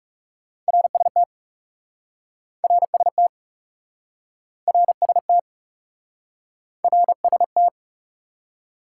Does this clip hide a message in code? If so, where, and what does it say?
Yes: Morse code from 0.7 to 7.7 s, heard as a keyed tone, about 40 wpm, 700 Hz. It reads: RST RST RST RST